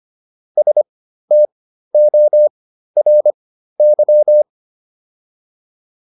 Morse code, heard as a keyed tone, about 25 wpm, spaced as for 13 wpm, 600 Hz. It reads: STORY
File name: PR405 13